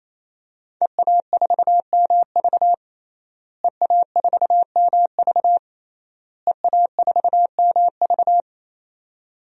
Morse code, heard as a keyed tone, about 28 wpm, 700 Hz. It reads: EA4MV EA4MV EA4MV
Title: DXCC-100 Call Signs - 5 Characters - Repeated 3x - 28wpm